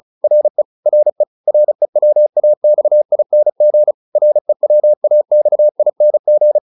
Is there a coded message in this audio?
Yes: Morse code from 0.2 to 6.6 s, heard as a keyed tone, about 34 wpm, 600 Hz. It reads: RE RE REWAXING REWAXING